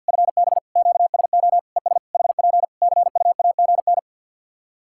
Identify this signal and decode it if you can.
Morse code, heard as a keyed tone, about 50 wpm, 700 Hz. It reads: FB QSO ES HP CUAGN
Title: K1USN SST QSO Practice 50wpm